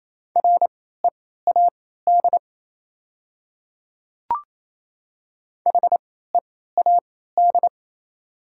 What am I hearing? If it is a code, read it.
Morse code, heard as a keyed tone, about 28 wpm, spaced as for 16 wpm, 700 Hz. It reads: READ HEAD
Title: 3x Character Spacing - Top 300 Words - 28wpm